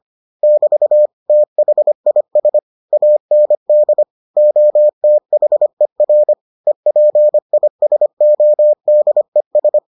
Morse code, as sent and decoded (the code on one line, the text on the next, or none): -...- / - .... .. ... / .- -. -.. / --- - .... .-. / . .--. .. ... --- -.. ...
<BT> THIS AND OTHER EPISODES